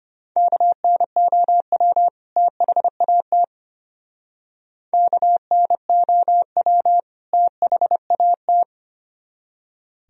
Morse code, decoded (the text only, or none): KNOW THAT KNOW THAT